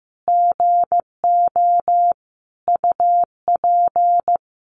Morse code, heard as a keyed tone, about 15 wpm, 700 Hz. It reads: GO UP
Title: Additional Easy Phrases 15wpm